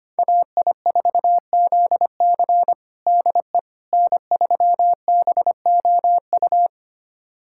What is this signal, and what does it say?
Morse code, heard as a keyed tone, about 25 wpm, 700 Hz. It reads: AI4ZC DE N3BOU
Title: USA Callsign DE Callsign - 5 Characters - 25wpm